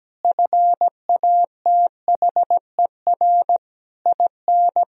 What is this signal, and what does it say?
Morse code, heard as a keyed tone, about 17 wpm, 700 Hz. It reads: FATHER IN
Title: Sentences from Top 1000 Words - Review Part 4 - 17wpm